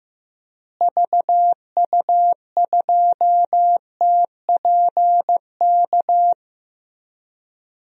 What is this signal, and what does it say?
Morse code, heard as a keyed tone, about 15 wpm, 700 Hz. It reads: VU2TPK
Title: DXCC-100 Call Signs - 6 Characters - Repeated 3x - 15wpm